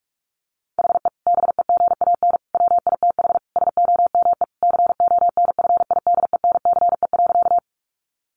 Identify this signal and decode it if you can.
Morse code, heard as a keyed tone, about 44 wpm, 700 Hz. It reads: HE BEGAN WITH SOME CONFIDENCE.
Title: Sentences from Top 1500 Words 45wpm